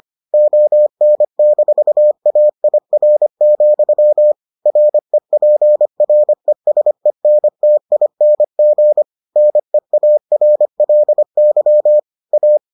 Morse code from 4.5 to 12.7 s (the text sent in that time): REPRESENTING NEARLY A